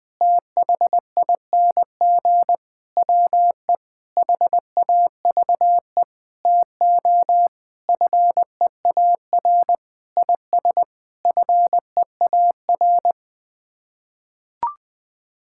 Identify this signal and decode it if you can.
Morse code, heard as a keyed tone, about 21 wpm, 700 Hz. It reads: THING WE HAVE TO FEAR IS FEAR
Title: Sets of 9 Words - 20wpm